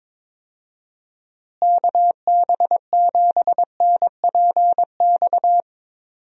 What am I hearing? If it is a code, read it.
Morse code, heard as a keyed tone, about 22 wpm, 700 Hz. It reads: KB7NPX
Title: USA Call Signs - 6 Characters - Repeated 3x - 22wpm